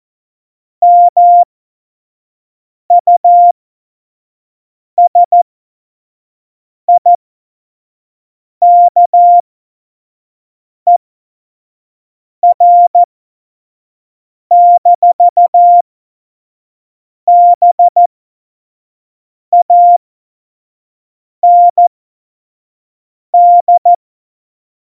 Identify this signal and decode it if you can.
Morse code, heard as a keyed tone, about 14 wpm, spaced as for 5 wpm, 700 Hz. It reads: MUSIKER-BAND
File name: Morsekurs / Witze / Witz 04. Dezember 2018, 25 F70